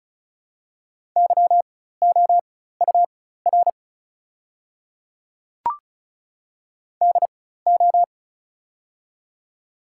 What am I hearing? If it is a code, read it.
Morse code, heard as a keyed tone, about 36 wpm, spaced as for 16 wpm, 700 Hz. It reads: YOUR DO
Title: 4x Character Spacing - Top 100 Words - 35wpm